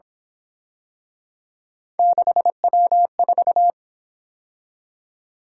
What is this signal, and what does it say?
Morse code, heard as a keyed tone, about 27 wpm, 700 Hz. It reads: 6W4
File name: Speed-Racing - Three Letter-Number Combinations - 22wpm